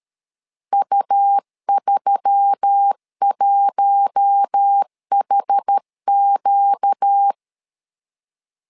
Morse code, heard as a keyed tone, about 13 wpm, 800 Hz. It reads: U31HQ
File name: LEZIONI / 4 Q1S3 12wpm 2023